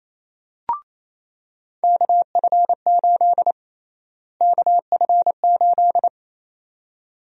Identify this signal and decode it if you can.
Morse code, heard as a keyed tone, about 28 wpm, 700 Hz. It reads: KF8 KF8
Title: USA Call Sign Prefixes - Repeated 2x - 28wpm